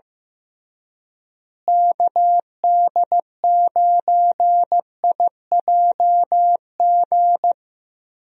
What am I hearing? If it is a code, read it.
Morse code, heard as a keyed tone, about 15 wpm, 700 Hz. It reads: KD9IJG
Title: USA Call Signs - 6 Characters - Repeated 3x - 15wpm